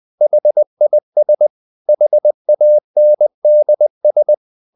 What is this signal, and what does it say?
Morse code, heard as a keyed tone, about 20 wpm, 600 Hz. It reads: HIS HANDS
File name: the old man and the sea part 2 20wpm